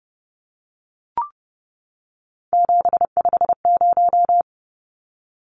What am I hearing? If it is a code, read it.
Morse code, heard as a keyed tone, about 30 wpm, 700 Hz. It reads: 750
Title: Numbers - 3-Digit 30wpm